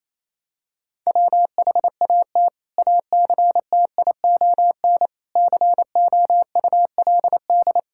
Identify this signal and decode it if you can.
Morse code, heard as a keyed tone, about 28 wpm, 700 Hz. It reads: WHAT ACTION COULD
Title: Sentences from Top 1000 Words - Review Part 6 - 28wpm